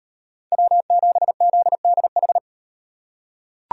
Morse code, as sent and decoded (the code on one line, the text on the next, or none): .-- --... --.. -.. ....
W7ZDH